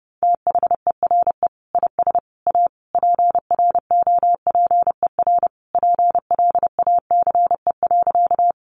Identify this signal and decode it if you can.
Morse code, heard as a keyed tone, about 30 wpm, 700 Hz. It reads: THERE IS A PROPER PLACE.
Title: Sentences from Top 800 Words 30wpm